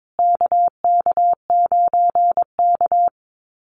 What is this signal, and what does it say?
Morse code, heard as a keyed tone, about 22 wpm, 700 Hz. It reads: KK9K
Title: USA Call Signs Repeated 3x 22wpm